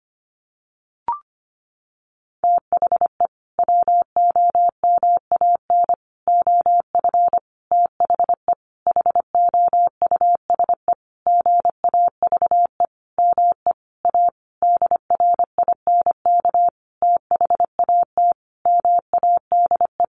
Morse code, heard as a keyed tone, about 25 wpm, 700 Hz. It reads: THE WOMAN OF THE HOUSE GAVE ME A DRINK THAT MADE
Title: Sentences from Top 1500 Words 25wpm